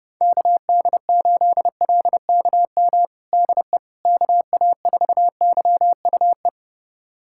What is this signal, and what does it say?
Morse code, heard as a keyed tone, about 32 wpm, 700 Hz. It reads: KD8LKM DE KA4YUE
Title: USA Callsign DE Callsign - 6 Characters - 30wpm